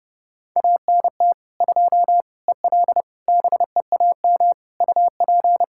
Morse code, heard as a keyed tone, about 30 wpm, 700 Hz. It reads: ANT 2 EL BEAM UP
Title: Practice QSO - Exchange 2 - Info - 30wpm